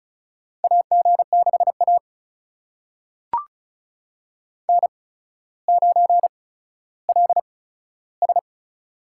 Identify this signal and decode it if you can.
Morse code, heard as a keyed tone, about 36 wpm, spaced as for 15 wpm, 700 Hz. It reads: AG6A N9LS
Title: USA Call Signs - 8x Character Spacing - 35wpm